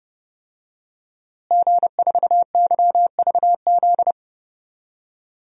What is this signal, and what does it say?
Morse code, heard as a keyed tone, about 30 wpm, 700 Hz. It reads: G4YVZ
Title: DXCC-100 Call Signs - 5 Characters - Repeated 2x - 30wpm